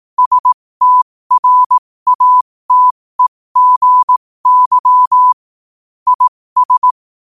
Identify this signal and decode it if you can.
Morse code, heard as a keyed tone, about 18 wpm, spaced as for 15 wpm, 1000 Hz. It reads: STRATEGY IS